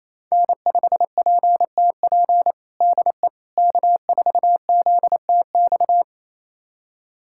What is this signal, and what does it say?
Morse code, heard as a keyed tone, about 27 wpm, 700 Hz. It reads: N5PTP DE K4ZTX
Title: USA Callsign DE Callsign - 5 Characters - 28wpm